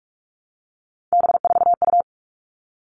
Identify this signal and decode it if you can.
Morse code, heard as a keyed tone, about 44 wpm, 700 Hz. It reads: B4U
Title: All Cut Numbers - 3-Digit 45wpm